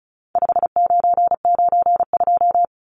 Morse code, heard as a keyed tone, about 34 wpm, 700 Hz. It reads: 5992
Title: Numbers - 4-Digit 35wpm